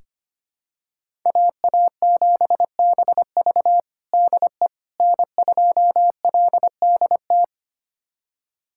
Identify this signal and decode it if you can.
Morse code, heard as a keyed tone, about 25 wpm, 700 Hz. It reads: AA7BV DE N2LDT
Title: USA Callsign DE Callsign - 5 Characters - 25wpm